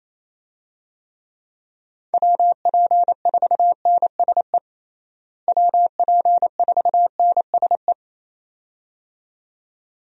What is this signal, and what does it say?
Morse code, heard as a keyed tone, about 28 wpm, 700 Hz. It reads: WP4NSE WP4NSE
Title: USA Call Signs - 6 Characters - Repeated 2x - 28wpm